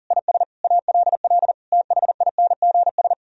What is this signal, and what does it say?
Morse code, heard as a keyed tone, about 52 wpm, spaced as for 46 wpm, 700 Hz. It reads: IS ALL THINGS
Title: Speed-Racing - Sets of 5 Words 40wpm